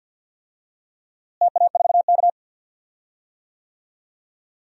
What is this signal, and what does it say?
Morse code, heard as a keyed tone, about 47 wpm, 700 Hz. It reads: TA4K
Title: DXCC-100 Call Signs - 4 Characters - Repeated 2x - 50wpm